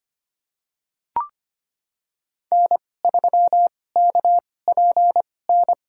Morse code, heard as a keyed tone, about 25 wpm, spaced as for 18 wpm, 700 Hz. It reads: N3KPN
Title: USA Call Signs - 2x Character Spacing - 25wpm